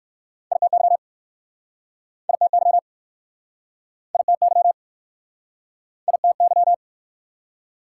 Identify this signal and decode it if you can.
Morse code, heard as a keyed tone, about 60 wpm, 700 Hz. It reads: ITY ITY ITY ITY